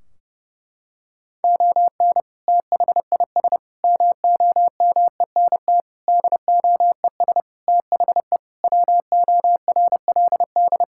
Morse code, heard as a keyed tone, about 32 wpm, 700 Hz. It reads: ON THIS MOMENT DOES THE WORLD
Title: Sentences from Top 1600 Words 30wpm